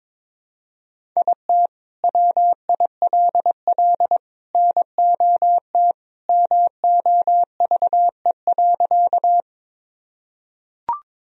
Morse code, heard as a keyed tone, about 22 wpm, 700 Hz. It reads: IT WILL NOT MOVE.
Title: Sentences from Top 1000 Words - Review Part 1 - 22wpm